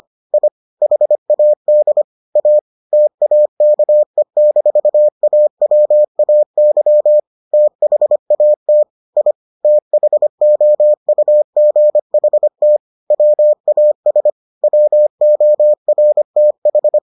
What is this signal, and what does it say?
Morse code, heard as a keyed tone, about 25 wpm, 600 Hz. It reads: I HAD A TAKE-AWAY THAT I THOUGHT WAS WORTH